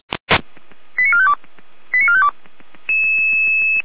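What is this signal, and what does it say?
Five-tone selective call (ZVEI), heard as quick runs of five tones, tones of about 70 ms. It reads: 89452 89452
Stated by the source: ZVEI five-tone call 89452 received by SDR, channel 455.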